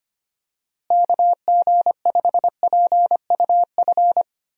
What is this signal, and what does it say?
Morse code, heard as a keyed tone, about 25 wpm, 700 Hz. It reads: KG5PUF
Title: USA Call Signs - 6 Characters - Repeated 2x - 25wpm